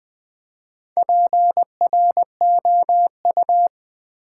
Morse code, heard as a keyed tone, about 20 wpm, 700 Hz. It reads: PROU
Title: Build A Word - 1 Letter At A Time - 3 to 5-Letter Words - 20wpm